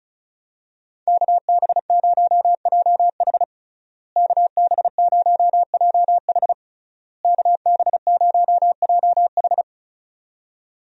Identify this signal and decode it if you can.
Morse code, heard as a keyed tone, about 36 wpm, 700 Hz. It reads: KB0JH KB0JH KB0JH
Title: USA Call Signs - 5 Characters - Repeated 3x - 35wpm